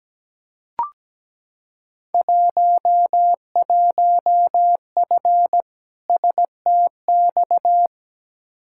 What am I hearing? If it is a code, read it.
Morse code, heard as a keyed tone, about 17 wpm, 700 Hz. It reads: 11F STX
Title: Field Day Exchanges Encore - 17wpm